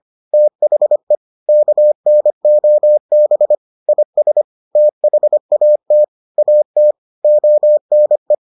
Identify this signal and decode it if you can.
Morse code, heard as a keyed tone, about 25 wpm, 600 Hz. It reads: THE KNOB IS THAT AT ONE